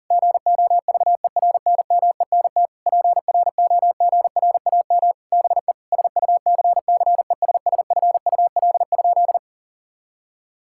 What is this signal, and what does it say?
Morse code, heard as a keyed tone, about 40 wpm, 700 Hz. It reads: GOVERNMENT PROGRAM BE SUCCESSFUL?
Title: Sentences from Top 1000 Words - Review Part 5 - 40wpm